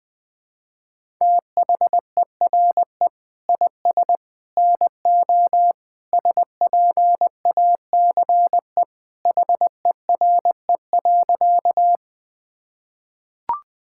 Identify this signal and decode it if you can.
Morse code, heard as a keyed tone, about 20 wpm, 700 Hz. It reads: THERE IS NO SPACE HERE.
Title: Sentences from Top 1000 Words - Review Part 5 - 20wpm